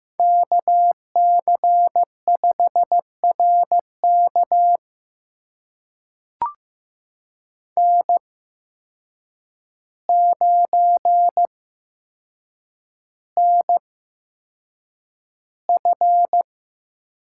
Morse code, heard as a keyed tone, about 15 wpm, spaced as for 7 wpm, 700 Hz. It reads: KC5RK N9NF
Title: USA Call Signs - 5 Characters - 8x Character Spacing - 15wpm